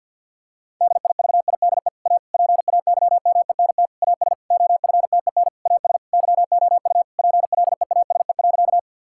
Morse code, heard as a keyed tone, about 51 wpm, 700 Hz. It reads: DIVIDE A PAYMENT AS OFTEN AS YOU PLEASE.
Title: Sentences from Top 1600 Words 50wpm